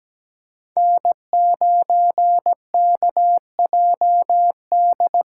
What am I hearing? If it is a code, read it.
Morse code, heard as a keyed tone, about 17 wpm, 700 Hz. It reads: N9KJD
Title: USA Call Signs - 5 Characters - 8x Character Spacing - 17wpm